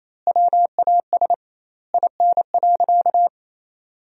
Morse code, heard as a keyed tone, about 28 wpm, 700 Hz. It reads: WAS IN.
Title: 2x Word-Spacing - Sentences from Top 400 Words - 28wpm